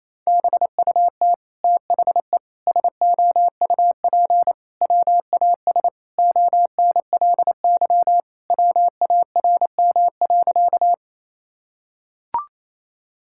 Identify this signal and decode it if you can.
Morse code, heard as a keyed tone, about 28 wpm, 700 Hz. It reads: BUT THE SOUP WAS ONLY WARM.
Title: Sentences from Top 1000 Words - Review Part 6 - 28wpm